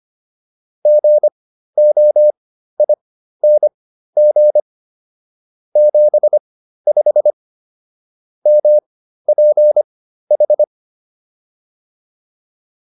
Morse code, heard as a keyed tone, about 25 wpm, spaced as for 13 wpm, 600 Hz. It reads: GOING 75 MPH
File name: PR402 13